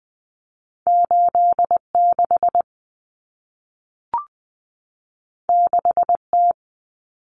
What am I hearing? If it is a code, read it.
Morse code, heard as a keyed tone, about 20 wpm, 700 Hz. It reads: 86 6T
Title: Common Cut Numbers - 2-Digit 20wpm